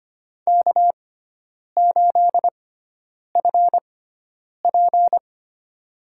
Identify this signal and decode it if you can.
Morse code, heard as a keyed tone, about 25 wpm, spaced as for 9 wpm, 700 Hz. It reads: K8FP
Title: USA Call Signs - 4 Characters - 6x Character Spacing - 25wpm